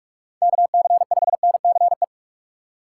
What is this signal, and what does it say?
Morse code, heard as a keyed tone, about 44 wpm, 700 Hz. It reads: KC5NCE
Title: USA Call Signs - 6 Characters - 4x Character Spacing - 45wpm